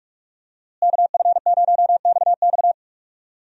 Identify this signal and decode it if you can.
Morse code, heard as a keyed tone, about 46 wpm, 700 Hz. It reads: KF0XX
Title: USA Call Signs - 5 Characters - Repeated 2x - 45wpm